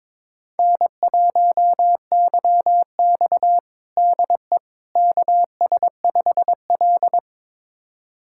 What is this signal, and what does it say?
Morse code, heard as a keyed tone, about 22 wpm, 700 Hz. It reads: N1YX DE KS5L
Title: USA Callsign DE Callsign - 4 Characters - 22wpm